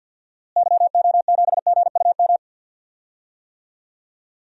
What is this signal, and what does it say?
Morse code, heard as a keyed tone, about 52 wpm, 700 Hz. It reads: YO7GUM
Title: DXCC-100 Call Signs - 6 Characters - Repeated 2x - 50wpm